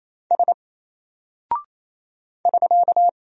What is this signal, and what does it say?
Morse code, heard as a keyed tone, about 28 wpm, 700 Hz. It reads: S <SK>
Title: Single Character (Letter:Number:Punctuation:Prosign) Alt 28wpm